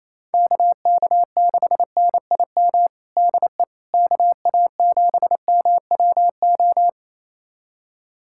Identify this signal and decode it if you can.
Morse code, heard as a keyed tone, about 28 wpm, 700 Hz. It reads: KK6NIM DE KA7MWO
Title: USA Callsign DE Callsign - 6 Characters - 28wpm